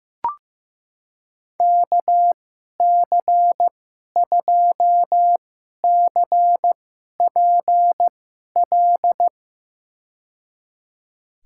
Morse code, heard as a keyed tone, about 15 wpm, spaced as for 11 wpm, 700 Hz. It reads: KC2CPL
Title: USA Call Signs - 6 Characters - 2x Character Spacing - 15wpm